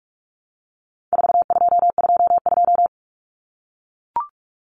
Morse code, heard as a keyed tone, about 44 wpm, 700 Hz. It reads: 4222
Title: Common Cut Numbers - 4-Digit 45wpm